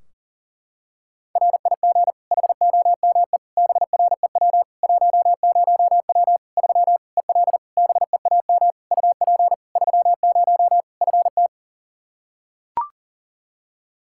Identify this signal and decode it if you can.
Morse code, heard as a keyed tone, about 40 wpm, 700 Hz. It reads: RIG HOME BREW 10W 3 EL BEAM UP 30 FT
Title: Practice QSO - Exchange 2 - Info - 40wpm